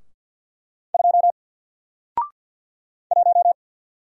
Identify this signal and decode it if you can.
Morse code, heard as a keyed tone, about 53 wpm, 700 Hz. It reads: 2 1
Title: Single Letter-Number 50wpm